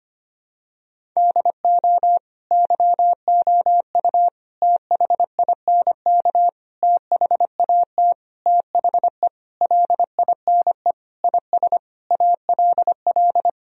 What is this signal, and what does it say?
Morse code, heard as a keyed tone, about 25 wpm, 700 Hz. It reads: DO YOU THINK THAT THE LINE IS ALL
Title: Sentences from Top 1000 Words - Review Part 1 - 25wpm